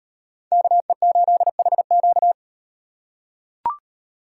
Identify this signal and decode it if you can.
Morse code, heard as a keyed tone, about 36 wpm, 700 Hz. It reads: KE8HQ